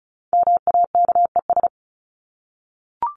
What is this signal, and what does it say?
Morse code, heard as a keyed tone, about 34 wpm, 700 Hz. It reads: MAKES
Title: ES:S Suffix - Top 100 Words with Base Words 35wpm